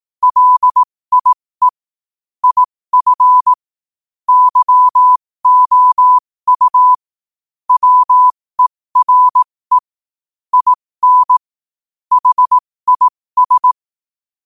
Morse code, heard as a keyed tone, about 18 wpm, spaced as for 15 wpm, 1000 Hz. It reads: LIE IF YOU WERE IN HIS
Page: New quotes every day in morse code at 15 Words per minute.